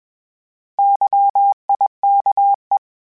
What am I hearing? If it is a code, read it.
Morse code, heard as a keyed tone, about 21 wpm, 800 Hz. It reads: YIKE
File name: Misc / Misc-mp3s / Misc @20